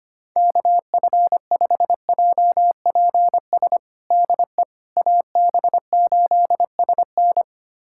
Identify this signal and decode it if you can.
Morse code, heard as a keyed tone, about 25 wpm, 700 Hz. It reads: KF5JPS DE AB8SN
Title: USA Callsign DE Callsign 25wpm